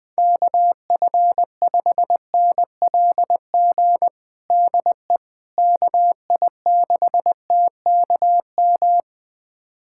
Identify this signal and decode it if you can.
Morse code, heard as a keyed tone, about 20 wpm, 700 Hz. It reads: KF5NLG DE KI6TKM
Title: USA Callsign DE Callsign - 6 Characters - 20wpm